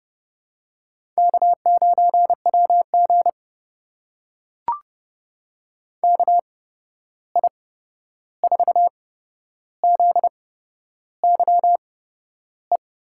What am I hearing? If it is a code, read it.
Morse code, heard as a keyed tone, about 30 wpm, spaced as for 11 wpm, 700 Hz. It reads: K9WG KI4ZYE
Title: USA Call Signs - 8x Character Spacing - 30wpm